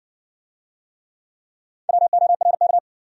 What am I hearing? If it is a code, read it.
Morse code, heard as a keyed tone, about 60 wpm, 700 Hz. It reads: WORK